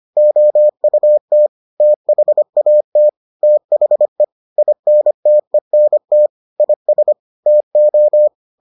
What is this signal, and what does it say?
Morse code, heard as a keyed tone, about 25 wpm, 600 Hz. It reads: OUT THAT THE INTENT IS TO